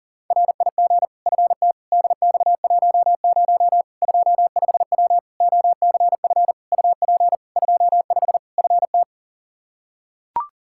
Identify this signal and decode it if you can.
Morse code, heard as a keyed tone, about 40 wpm, 700 Hz. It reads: RIG FT DX10 25W OCF UP 25 FT
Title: Practice QSO - Exchange 2 - Info - 40wpm